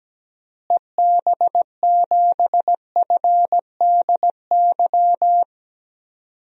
Morse code, heard as a keyed tone, about 17 wpm, 700 Hz. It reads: EB7FDY
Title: DXCC-100 Call Signs - 6 Characters - Repeated 3x - 17wpm